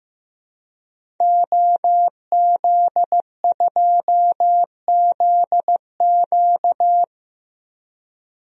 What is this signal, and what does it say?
Morse code, heard as a keyed tone, about 15 wpm, 700 Hz. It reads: OZ2ZQ
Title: DXCC-100 Call Signs - 5 Characters - 15wpm